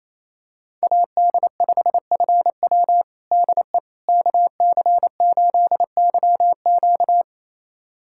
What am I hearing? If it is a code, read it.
Morse code, heard as a keyed tone, about 27 wpm, 700 Hz. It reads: AD5FW DE KC8YQ
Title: USA Callsign DE Callsign - 5 Characters - 28wpm